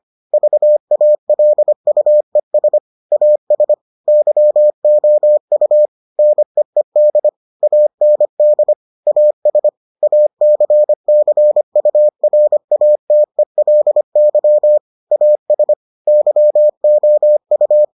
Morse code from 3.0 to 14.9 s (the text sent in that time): AS YOU NEED AND AS ACCURATELY